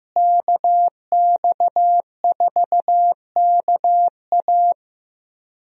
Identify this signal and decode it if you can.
Morse code, heard as a keyed tone, about 16 wpm, 700 Hz. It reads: KX4KA
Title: USA Call Signs - 5 Characters - Repeated 2x - 15wpm